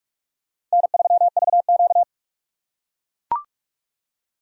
Morse code, heard as a keyed tone, about 41 wpm, 700 Hz. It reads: N3VQ
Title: USA Call Signs - 4 Characters - 6x Character Spacing - 45wpm